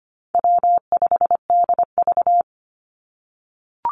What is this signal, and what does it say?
Morse code, heard as a keyed tone, about 25 wpm, 700 Hz. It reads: W5DV
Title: Easy Call Signs - 25wpm